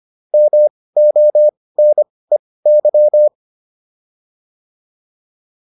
Morse code, heard as a keyed tone, about 25 wpm, spaced as for 18 wpm, 600 Hz. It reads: MONEY